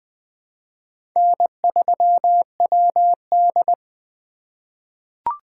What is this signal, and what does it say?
Morse code, heard as a keyed tone, about 20 wpm, 700 Hz. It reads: N3WD
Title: 2023 POTA Activator Call Signs - 20wpm